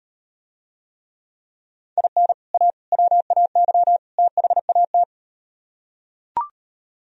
Morse code, heard as a keyed tone, about 37 wpm, 700 Hz. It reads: IN A WAY THAT